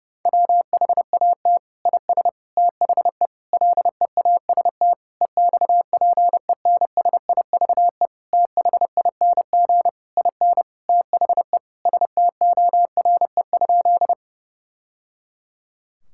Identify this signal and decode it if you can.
Morse code, heard as a keyed tone, about 32 wpm, 700 Hz. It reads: WHAT IS THE LEAST EXPENSIVE THING IN THE STORE?
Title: Sentences from Top 1000 Words - Review Part 6 - 30wpm